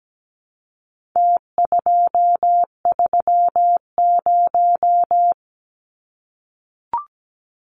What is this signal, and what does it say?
Morse code, heard as a keyed tone, about 17 wpm, 700 Hz. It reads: T230
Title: Numbers - Contest Style - 4-Digit 17wpm